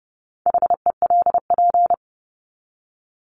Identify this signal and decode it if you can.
Morse code, heard as a keyed tone, about 30 wpm, 700 Hz. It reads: HELP
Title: Top 200 Words 30wpm